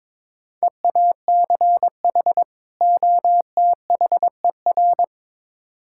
Morse code, heard as a keyed tone, about 22 wpm, 700 Hz. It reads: EACH OTHER
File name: Build A Phrase - 2-Word Phrases - 22wpm